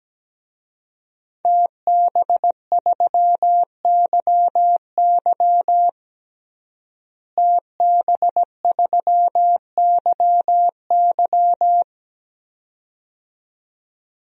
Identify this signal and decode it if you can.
Morse code, heard as a keyed tone, about 17 wpm, 700 Hz. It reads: TB3YY TB3YY
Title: DXCC-100 Call Signs - 5 Characters - Repeated 2x - 17wpm